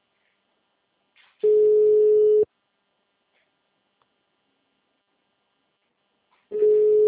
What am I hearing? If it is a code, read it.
Ringback tone (Europe, 425 Hz).